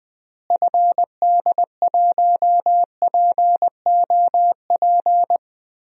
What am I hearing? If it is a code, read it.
Morse code, heard as a keyed tone, about 20 wpm, 700 Hz. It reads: FD1POP
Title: DXCC-100 Call Signs - 6 Characters - Repeated 2x - 20wpm